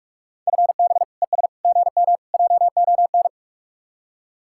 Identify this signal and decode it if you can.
Morse code, heard as a keyed tone, about 44 wpm, 700 Hz. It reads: FB ES GM JON